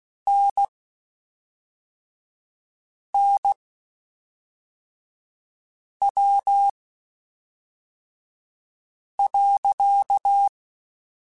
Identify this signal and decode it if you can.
Morse code, heard as a keyed tone, about 16 wpm, spaced as for 3 wpm, 800 Hz. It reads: NNW.
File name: Spaced Characters / Spaced Characters-mp3s / Weather5 22 sp @5